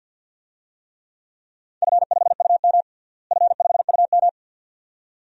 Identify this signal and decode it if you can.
Morse code, heard as a keyed tone, about 52 wpm, 700 Hz. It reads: F5UM F5UM
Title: DXCC-100 Call Signs - 4 Characters - Repeated 2x - 50wpm